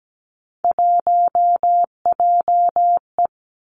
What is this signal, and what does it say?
Morse code, heard as a keyed tone, about 17 wpm, 700 Hz. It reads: 1JE
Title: Three Challenging Letter-Number Combinations 17wpm